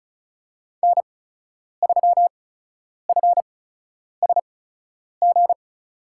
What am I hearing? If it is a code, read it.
Morse code, heard as a keyed tone, about 34 wpm, spaced as for 10 wpm, 700 Hz. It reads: N3FSG
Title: USA Call Signs - 5 Characters - 8x Character Spacing - 35wpm